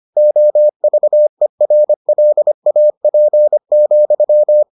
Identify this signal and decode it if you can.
Morse code, heard as a keyed tone, about 25 wpm, 600 Hz. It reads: OVERLAP,